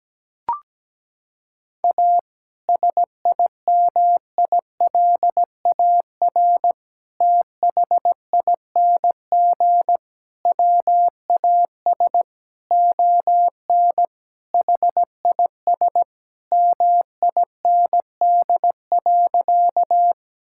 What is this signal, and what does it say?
Morse code, heard as a keyed tone, about 17 wpm, 700 Hz. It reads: A SIMILAR THING WAS ON HIS MIND.
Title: Sentences from Top 1000 Words - Review Part 5 - 17wpm